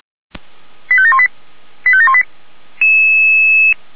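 Five-tone selective call (ZVEI), heard as quick runs of five tones, tones of about 70 ms. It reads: 85718 85718
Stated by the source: ZVEI five-tone selective call received on an SDR.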